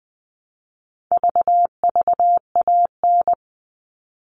Morse code, heard as a keyed tone, about 20 wpm, 700 Hz. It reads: VVAN
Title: All Cut Numbers - 4-Digit 20wpm